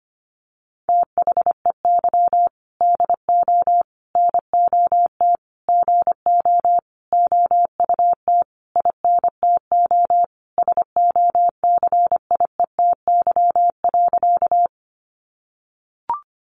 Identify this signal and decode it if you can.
Morse code, heard as a keyed tone, about 25 wpm, 700 Hz. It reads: THEY DO NOT GO OUT INTO SOCIETY.
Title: Sentences from Top 700 Words 25wpm